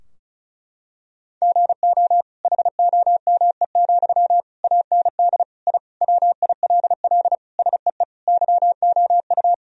Morse code, heard as a keyed tone, about 34 wpm, 700 Hz. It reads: GO HOME, AND I WILL SEE YOU
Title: Sentences from Top 1000 Words - Review Part 4 - 35wpm